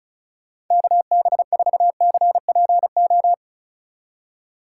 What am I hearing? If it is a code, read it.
Morse code, heard as a keyed tone, about 43 wpm, 700 Hz. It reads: KB4CPO
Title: Speed Racing - USA 2x3 Call Signs 35wpm